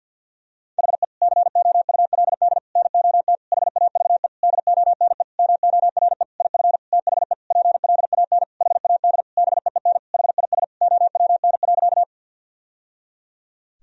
Morse code, heard as a keyed tone, about 52 wpm, 700 Hz. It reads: HE COULD NOT HAVE DONE MORE IF THE PLAN HAD BEEN HIS OWN.
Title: Sentences from Top 1000 Words - Review Part 2 - 50wpm